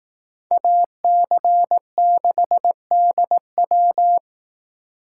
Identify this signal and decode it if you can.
Morse code, heard as a keyed tone, about 18 wpm, 700 Hz. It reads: AC6DW
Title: Speed Racing - USA 2x2 Call Signs 15wpm